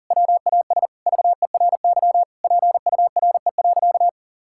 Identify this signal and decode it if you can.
Morse code, heard as a keyed tone, about 40 wpm, 700 Hz. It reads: WAS VERY PURE.
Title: Sentences from Top 1600 Words 40wpm